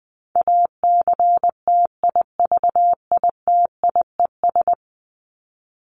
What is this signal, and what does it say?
Morse code, heard as a keyed tone, about 19 wpm, 700 Hz. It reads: ACTIVITIES
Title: ES:S Suffix - Top 100 Words with Base Words 20wpm